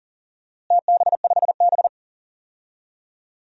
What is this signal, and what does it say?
Morse code, heard as a keyed tone, about 37 wpm, 700 Hz. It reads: TB5B
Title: DXCC-100 Call Signs - 4 Characters - Repeated 3x - 40wpm